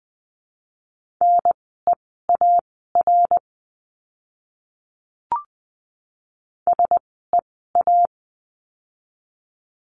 Morse code, heard as a keyed tone, about 20 wpm, spaced as for 15 wpm, 700 Hz. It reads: NEAR SEA
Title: 061 Lesson 11f - Letter h - Words 20wpm